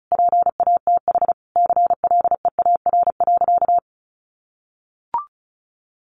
Morse code, heard as a keyed tone, about 34 wpm, 700 Hz. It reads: PATH CLEAR.
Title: Sentences from Top 1400 Words 35wpm